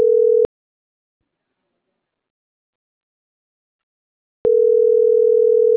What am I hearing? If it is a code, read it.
Ringback tone (North America), 440 + 480 Hz.